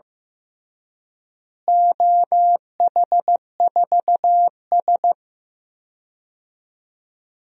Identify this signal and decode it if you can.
Morse code, heard as a keyed tone, about 15 wpm, 700 Hz. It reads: OH4S
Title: DXCC-100 Call Signs - 4 Characters - Repeated 3x - 15wpm